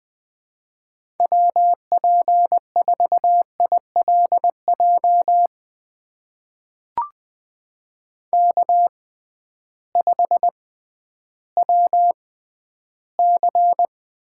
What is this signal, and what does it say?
Morse code, heard as a keyed tone, about 20 wpm, spaced as for 12 wpm, 700 Hz. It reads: WP4ILJ K5WC
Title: USA Call Signs - 6x Character Spacing - 20wpm